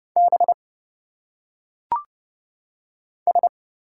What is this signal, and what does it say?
Morse code, heard as a keyed tone, about 30 wpm, 700 Hz. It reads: B S
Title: Single Letters - Phonetic Pronunciation 30wpm